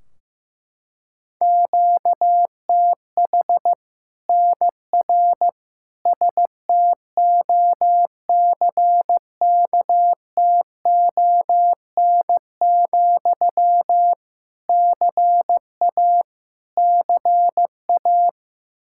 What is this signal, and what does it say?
Morse code, heard as a keyed tone, about 15 wpm, 700 Hz. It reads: QTH NR STOCKTON, CA CA
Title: Practice QSO - Exchange 1 - Beginning - 15wpm